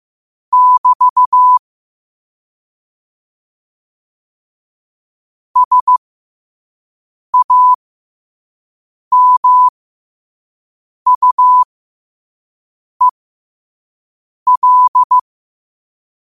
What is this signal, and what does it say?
Morse code, heard as a keyed tone, about 15 wpm, spaced as for 5 wpm, 1000 Hz. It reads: <BT> SAMUEL
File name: qotdfiles / 5wpm